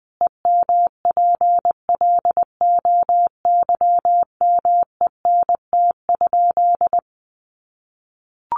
Morse code, heard as a keyed tone, about 20 wpm, 700 Hz. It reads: EMPLOYMENT?
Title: Sentences from Top 1500 Words 20wpm